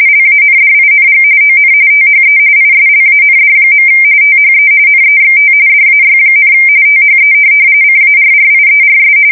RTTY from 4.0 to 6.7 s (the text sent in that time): THE NEXT DAY,